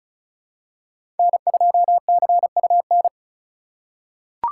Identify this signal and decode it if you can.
Morse code, heard as a keyed tone, about 36 wpm, 700 Hz. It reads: N2CUN
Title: USA Call Signs - 5 Characters - 3x Character Spacing - 35wpm